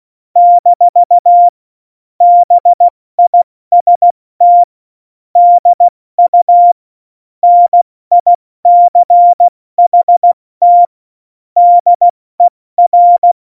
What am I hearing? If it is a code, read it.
Morse code, heard as a keyed tone, about 16 wpm, 700 Hz. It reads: - BIST DU NICHT DER
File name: Morsekurs / Witze / Witz 26. Januar 2021, 70 F80